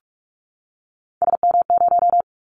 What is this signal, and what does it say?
Morse code, heard as a keyed tone, about 46 wpm, 700 Hz. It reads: SM0
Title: Callsign Prefixes for Top 25 DXCC - 45wpm